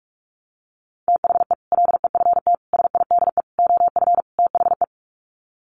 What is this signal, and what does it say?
Morse code, heard as a keyed tone, about 44 wpm, 700 Hz. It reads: THE LEFT SIDE OF THE
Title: Sets of 5 Words Encore 45wpm